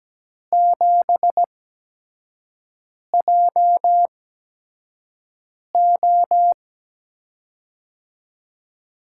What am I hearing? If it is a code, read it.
Morse code, heard as a keyed tone, about 17 wpm, spaced as for 5 wpm, 700 Hz. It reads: 7JO
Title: USA Call Signs - 8x Character Spacing - 17wpm